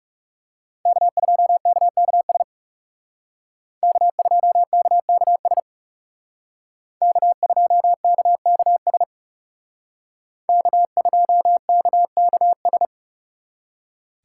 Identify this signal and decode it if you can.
Morse code, heard as a keyed tone, about 34 wpm, 700 Hz. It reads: K2KKS K2KKS K2KKS K2KKS
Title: Speed Racing - USA 1x3 Call Signs 30wpm